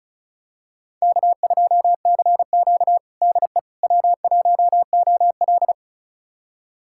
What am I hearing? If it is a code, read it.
Morse code, heard as a keyed tone, about 36 wpm, 700 Hz. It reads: K2CQ DE W1OL
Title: USA Callsign DE Callsign - 4 Characters - 35wpm